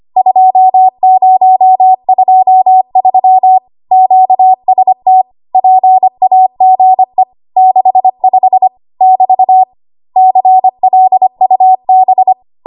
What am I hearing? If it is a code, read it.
Morse code, heard as a keyed tone, about 25 wpm, 750 Hz. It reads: 2023 QST PAGE 65 <BT> CLUB
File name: Morse / 25WPM